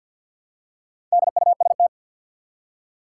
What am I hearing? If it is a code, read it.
Morse code, heard as a keyed tone, about 47 wpm, 700 Hz. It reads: DUST
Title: Four Letter Words - 50wpm